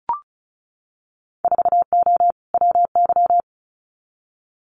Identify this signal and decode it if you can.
Morse code, heard as a keyed tone, about 36 wpm, 700 Hz. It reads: 4O WY
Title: Winter Field Day Exchanges Encore 35wpm